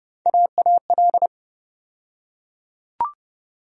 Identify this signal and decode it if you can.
Morse code, heard as a keyed tone, about 32 wpm, 700 Hz. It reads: AAL
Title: USA Call Sign Postfixes - 3 Letters 30wpm